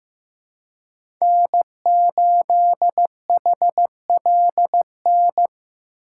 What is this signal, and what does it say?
Morse code, heard as a keyed tone, about 15 wpm, 700 Hz. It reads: N8HLN
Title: USA Call Signs - 5 Characters - 3x Character Spacing - 15wpm